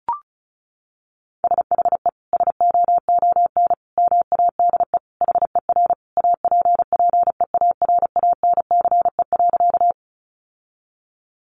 Morse code, heard as a keyed tone, about 34 wpm, 700 Hz. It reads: SHE SOON MADE HER APPEARANCE.
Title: Sentences from Top 1100 Words 35wpm